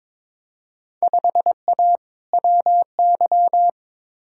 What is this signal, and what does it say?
Morse code, heard as a keyed tone, about 21 wpm, 700 Hz. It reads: 5A WY
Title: Field Day Exchanges Encore - 22wpm